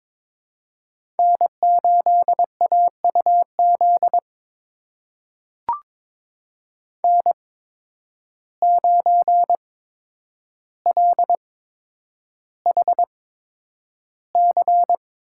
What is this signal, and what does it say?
Morse code, heard as a keyed tone, about 22 wpm, spaced as for 10 wpm, 700 Hz. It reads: N8AUZ N9LHC
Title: USA Call Signs - 5 Characters - 8x Character Spacing - 22wpm